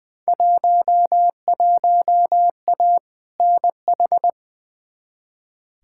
Morse code, heard as a keyed tone, about 21 wpm, 700 Hz. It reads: 11A NH
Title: Field Day Exchanges Encore - 20wpm